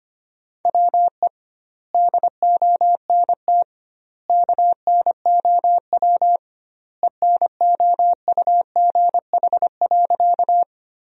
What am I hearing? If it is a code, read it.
Morse code, heard as a keyed tone, about 25 wpm, 700 Hz. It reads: WE DONT KNOW ENOUGH.
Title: 2x Word-Spacing - Sentences from Top 400 Words - 25wpm